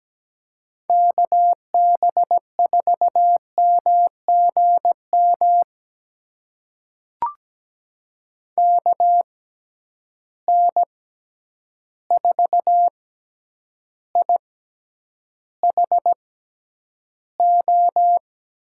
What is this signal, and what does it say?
Morse code, heard as a keyed tone, about 17 wpm, spaced as for 9 wpm, 700 Hz. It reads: KB4MGM KN4IHO
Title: USA Call Signs - 6 Characters - 6x Character Spacing - 17wpm